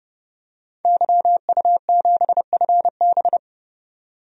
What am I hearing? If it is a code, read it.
Morse code, heard as a keyed tone, about 31 wpm, 700 Hz. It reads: YU7FB
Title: DXCC-100 Call Signs - 5 Characters - 30wpm